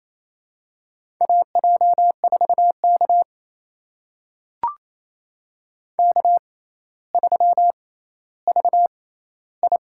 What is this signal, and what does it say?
Morse code, heard as a keyed tone, about 28 wpm, spaced as for 14 wpm, 700 Hz. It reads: AJ4K K3VI
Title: USA Call Signs - 4 Characters - 6x Character Spacing - 28wpm